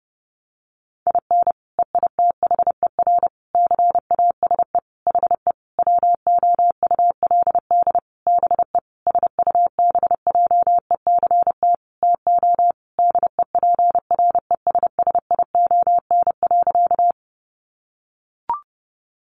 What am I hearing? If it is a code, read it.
Morse code, heard as a keyed tone, about 28 wpm, 700 Hz. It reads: IN EITHER CASE HE WOULD BE SUBJECT TO DEPRESSION.
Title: Sentences from Top 1300 Words 30wpm